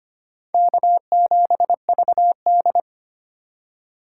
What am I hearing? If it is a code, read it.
Morse code, heard as a keyed tone, about 25 wpm, 700 Hz. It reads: K7VD
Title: USA Call Signs - 4 Characters - 2x Character Spacing - 25wpm